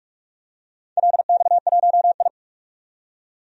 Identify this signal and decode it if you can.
Morse code, heard as a keyed tone, about 44 wpm, 700 Hz. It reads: LX1I
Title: DXCC-100 Call Signs - 4 Characters - 45wpm